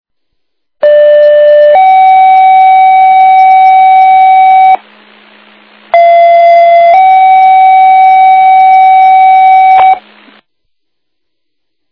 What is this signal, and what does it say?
Two-tone page, heard as a one-second tone, then a three-second tone, about 590 then 760 Hz, about 680 then 760 Hz. It is heard from 0.8 to 9.9 s.